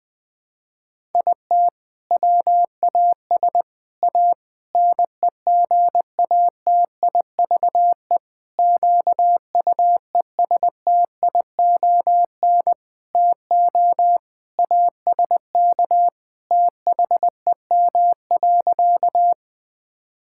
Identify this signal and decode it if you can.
Morse code, heard as a keyed tone, about 20 wpm, 700 Hz. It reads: IT WAS A NEGATIVE QUESTION TO ASK THEM.
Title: Sentences from Top 1000 Words - Review Part 6 - 20wpm